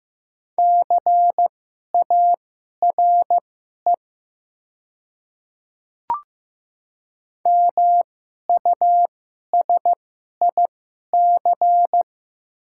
Morse code, heard as a keyed tone, about 15 wpm, spaced as for 11 wpm, 700 Hz. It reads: CARE MUSIC
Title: 2x Character Spacing - Top 300 Words - 15wpm